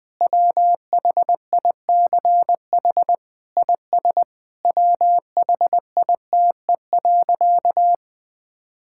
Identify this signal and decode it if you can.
Morse code, heard as a keyed tone, about 20 wpm, 700 Hz. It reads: WHICH IS WHITE.
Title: Sentences from Top 1600 Words 20wpm